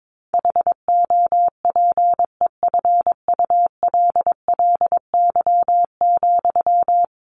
Morse code, heard as a keyed tone, about 22 wpm, 700 Hz. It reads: HOPEFULLY,
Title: Sentences from Top 1500 Words 22wpm